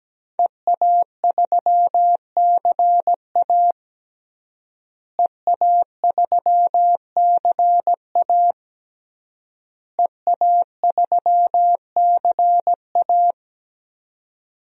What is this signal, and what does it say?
Morse code, heard as a keyed tone, about 17 wpm, 700 Hz. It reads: EA3CA EA3CA EA3CA
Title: DXCC-100 Call Signs - 5 Characters - Repeated 3x - 17wpm